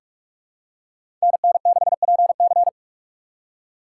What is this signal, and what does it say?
Morse code, heard as a keyed tone, about 41 wpm, 700 Hz. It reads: NN6PC
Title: USA Call Signs - 3x Character Spacing - 45wpm